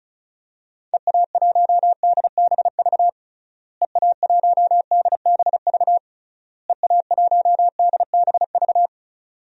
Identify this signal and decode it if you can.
Morse code, heard as a keyed tone, about 36 wpm, 700 Hz. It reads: EA1DBV EA1DBV EA1DBV
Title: DXCC-100 Call Signs - 6 Characters - Repeated 3x - 35wpm